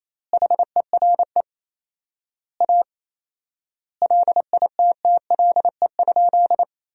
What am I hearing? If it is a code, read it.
Morse code, heard as a keyed tone, about 28 wpm, 700 Hz. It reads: HERE A LITTLE?
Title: 4x Word-Spacing - Sentences from Top 500 Words - 28wpm